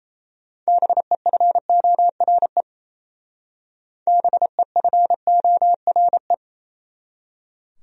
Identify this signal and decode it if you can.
Morse code, heard as a keyed tone, about 36 wpm, 700 Hz. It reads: BEFORE BEFORE